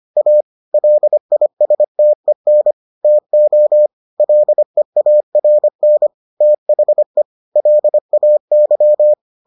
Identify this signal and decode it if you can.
Morse code, heard as a keyed tone, about 25 wpm, 600 Hz. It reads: A LISTEN TO LEARN THE LAY